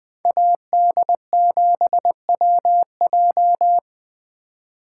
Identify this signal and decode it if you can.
Morse code, heard as a keyed tone, about 20 wpm, 700 Hz. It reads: AD7WJ
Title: Speed Racing - USA 2x2 Call Signs 17wpm